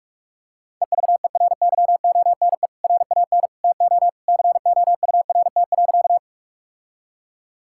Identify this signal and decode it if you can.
Morse code, heard as a keyed tone, about 46 wpm, 700 Hz. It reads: EVERYONE RAN TO COURT.
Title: Sentences from Top 1600 Words 45wpm